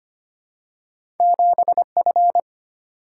Morse code, heard as a keyed tone, about 25 wpm, 700 Hz. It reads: 7F
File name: Two Letter-Number Combinations - 25wpm